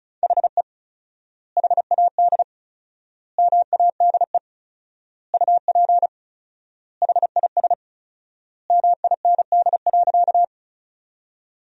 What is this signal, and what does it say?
Morse code, heard as a keyed tone, about 36 wpm, 700 Hz. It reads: HE HAD MADE UP HIS MIND.
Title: 4x Word-Spacing - Sentences from Top 500 Words - 35wpm